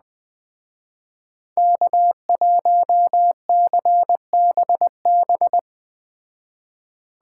Morse code, heard as a keyed tone, about 20 wpm, 700 Hz. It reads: K1CBB